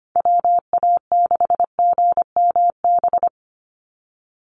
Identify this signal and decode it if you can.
Morse code, heard as a keyed tone, about 25 wpm, 700 Hz. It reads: WA6GMB
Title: USA Call Signs 25wpm